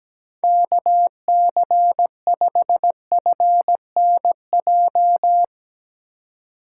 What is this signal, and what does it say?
Morse code, heard as a keyed tone, about 17 wpm, 700 Hz. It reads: KC5FNJ
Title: USA Call Signs - 6 Characters - 4x Character Spacing - 17wpm